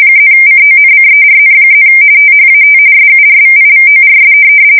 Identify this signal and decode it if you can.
RTTY, about 45.45 baud. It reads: +CONCERN+ THAT SURVIVORS
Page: 170 Shift 2125/2295